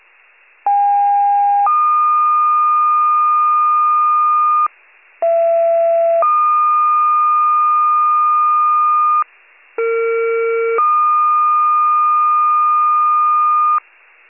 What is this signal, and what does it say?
Two-tone page, heard as a one-second tone, then a three-second tone, about 810 then 1220 Hz, about 670 then 1220 Hz, about 460 then 1220 Hz. It is heard from 0.7 to 13.8 s.
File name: wcalert